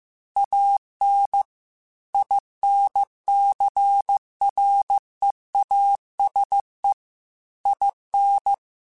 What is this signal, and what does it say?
Morse code, heard as a keyed tone, about 15 wpm, 800 Hz. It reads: AN INCREASE IN
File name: Technical Subjects / Technical Subjects-mp3s / XL @13